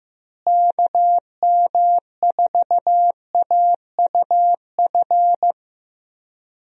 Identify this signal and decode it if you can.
Morse code, heard as a keyed tone, about 14 wpm, 700 Hz. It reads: KM4AUF
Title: USA Call Signs - 6 Characters - Repeated 3x - 15wpm